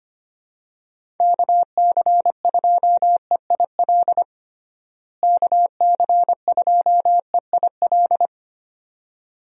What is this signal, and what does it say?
Morse code, heard as a keyed tone, about 25 wpm, 700 Hz. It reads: KC2EIL KC2EIL
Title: USA Call Signs - 6 Characters - Repeated 2x - 25wpm